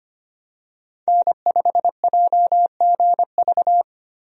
Morse code, heard as a keyed tone, about 25 wpm, 700 Hz. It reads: N5JGV
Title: USA Call Signs - 5 Characters - 25wpm